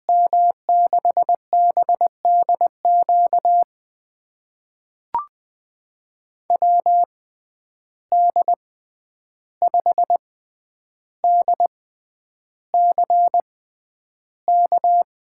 Morse code, heard as a keyed tone, about 20 wpm, spaced as for 10 wpm, 700 Hz. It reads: M6BDQ WD5DCK
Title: USA Call Signs - 6 Characters - 6x Character Spacing - 20wpm